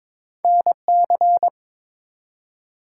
Morse code, heard as a keyed tone, about 21 wpm, 700 Hz. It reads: NC
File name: practice / 22wpm / ARRL Sections v2 - 22wpm